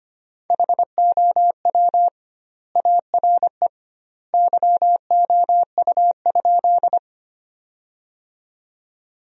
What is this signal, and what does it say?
Morse code, heard as a keyed tone, about 25 wpm, 700 Hz. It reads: HOW ARE YOU?
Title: 2x Word-Spacing - Sentences from Top 100 Words - 25wpm